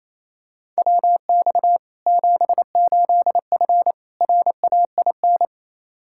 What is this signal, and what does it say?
Morse code, heard as a keyed tone, about 28 wpm, 700 Hz. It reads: WX 78F RAIN
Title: K1USN SST QSO Practice 28wpm